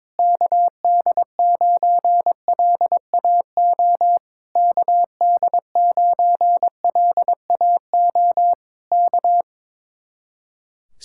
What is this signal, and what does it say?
Morse code, heard as a keyed tone, about 21 wpm, 700 Hz. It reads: KD9LAO KD9LAO K
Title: Speed-Racing Common QSO Phrases Encore - 22wpm